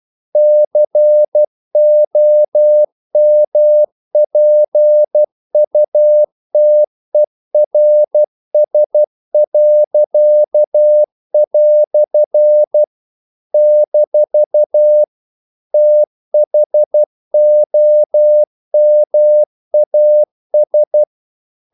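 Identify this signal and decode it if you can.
Morse code, heard as a keyed tone, about 12 wpm, 600 Hz. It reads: COMPUTERS." - THOMAS
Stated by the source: Morse practice sending English quotations.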